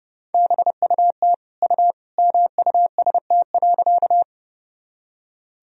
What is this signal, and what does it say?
Morse code, heard as a keyed tone, about 30 wpm, 700 Hz. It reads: BUT U MUST.
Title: Sentences from Top 200 Words and Abbreviations 30wpm